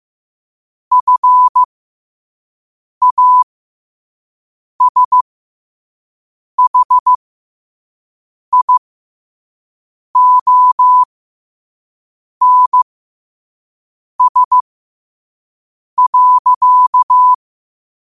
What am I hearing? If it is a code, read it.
Morse code, heard as a keyed tone, about 15 wpm, spaced as for 5 wpm, 1000 Hz. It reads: FASHIONS.
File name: qotdfiles / 5wpm